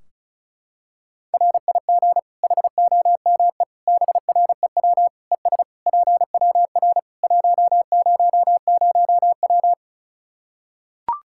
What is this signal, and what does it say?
Morse code, heard as a keyed tone, about 36 wpm, 700 Hz. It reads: RIG HOME BREW ES PWR 100W
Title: Practice QSO - Exchange 2 - Info - 35wpm